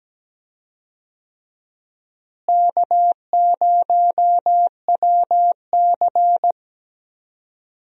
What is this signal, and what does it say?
Morse code, heard as a keyed tone, about 17 wpm, 700 Hz. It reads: K0WC
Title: USA Call Signs - 4 Characters - Repeated 3x - 17wpm